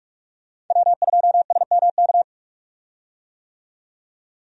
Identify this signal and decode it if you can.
Morse code, heard as a keyed tone, about 47 wpm, 700 Hz. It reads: W2SMK
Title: USA Call Signs - 5 Characters - 45wpm